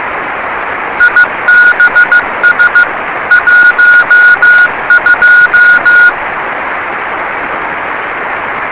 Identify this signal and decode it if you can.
Morse code, heard as a keyed tone, about 15 wpm, 1440 Hz. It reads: IBS 12
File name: rs30 cw